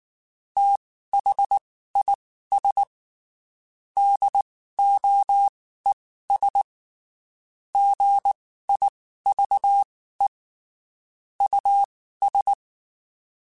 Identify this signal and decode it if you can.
Morse code, heard as a keyed tone, about 19 wpm, spaced as for 14 wpm, 800 Hz. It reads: THIS DOES GIVE US
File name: (spaced) / QSOs Spaced-mp3s / WD0FBT @18 sp